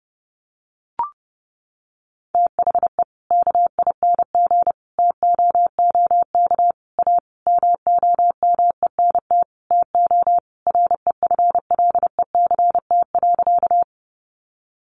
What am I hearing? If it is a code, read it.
Morse code, heard as a keyed tone, about 30 wpm, 700 Hz. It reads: THE KING TOOK A MOMENT TO REFLECT.
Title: Sentences from Top 1500 Words 30wpm